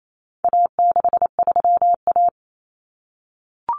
Morse code, heard as a keyed tone, about 28 wpm, 700 Hz. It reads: A63A
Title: Common Cut Numbers - 4-Digit 28wpm